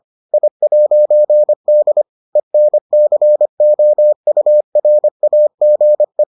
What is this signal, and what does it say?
Morse code, heard as a keyed tone, about 25 wpm, 600 Hz. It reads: I'D ENCOURAGE